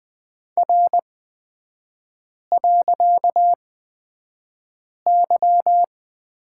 Morse code, heard as a keyed tone, about 20 wpm, spaced as for 5 wpm, 700 Hz. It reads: R.Y
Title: 255 Lesson 41e - Period - Three Characters 20wpm